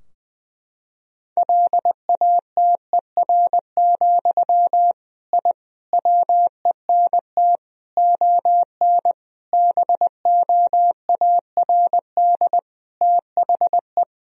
Morse code, heard as a keyed tone, about 20 wpm, 700 Hz. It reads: LATER, I WENT ON BOARD THE
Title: Sentences from Top 1000 Words - Review Part 4 - 20wpm